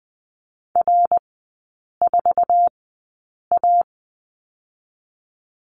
Morse code, heard as a keyed tone, about 20 wpm, spaced as for 8 wpm, 700 Hz. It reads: R4A
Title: 133 Lesson 22e - Question Mark - Three Characters 20wpm